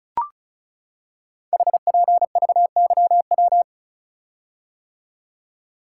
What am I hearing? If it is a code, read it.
Morse code, heard as a keyed tone, about 36 wpm, 700 Hz. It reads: HPVYW
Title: Code Groups - 5 Letters - 35wpm